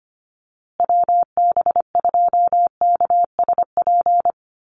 Morse code, heard as a keyed tone, about 25 wpm, 700 Hz. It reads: WB2KSP
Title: USA Call Signs Repeated 3x 25wpm